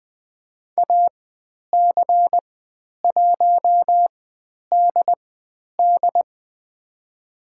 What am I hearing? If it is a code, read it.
Morse code, heard as a keyed tone, about 20 wpm, spaced as for 10 wpm, 700 Hz. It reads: AC1DD
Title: 095 Lesson 16g - Letter c - Call Signs 20wpm